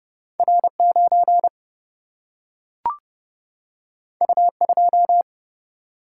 Morse code, heard as a keyed tone, about 30 wpm, 700 Hz. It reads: R 9 U 2
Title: Single-Letter Single-Number 30wpm